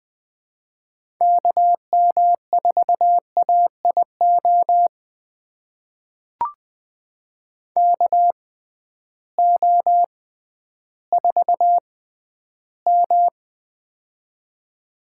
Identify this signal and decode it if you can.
Morse code, heard as a keyed tone, about 20 wpm, spaced as for 12 wpm, 700 Hz. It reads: KM4AIO KO4M
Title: USA Call Signs - 6x Character Spacing - 20wpm